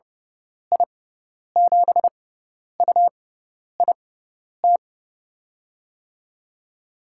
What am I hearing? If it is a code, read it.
Morse code, heard as a keyed tone, about 30 wpm, spaced as for 10 wpm, 700 Hz. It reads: I7UIT
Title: USA Call Signs - 6 Characters - 6x Character Spacing - 30wpm